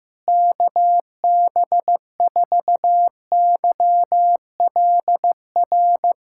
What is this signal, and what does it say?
Morse code, heard as a keyed tone, about 16 wpm, 700 Hz. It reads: KB4YLR
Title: USA Callsign DE Callsign - 6 Characters - 15wpm